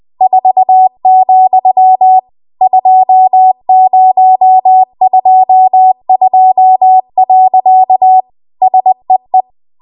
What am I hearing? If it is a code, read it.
Morse code, heard as a keyed tone, about 20 wpm, 750 Hz. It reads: 4, 2022. SEE
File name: Archive / 20 WPM / 20WPM